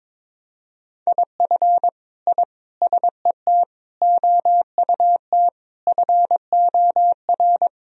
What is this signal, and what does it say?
Morse code, heard as a keyed tone, about 22 wpm, 700 Hz. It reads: IF I SET OUT FOR
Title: Sentences from Top 1000 Words - Review Part 2 - 22wpm